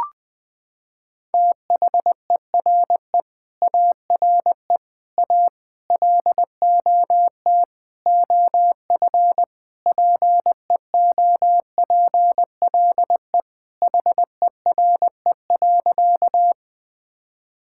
Morse code, heard as a keyed tone, about 21 wpm, 700 Hz. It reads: THERE ARE A LOT OF PEOPLE HERE.
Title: Sentences with 6-Word Phrases - 20wpm